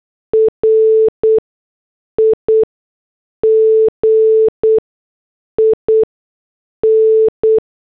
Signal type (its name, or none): Morse code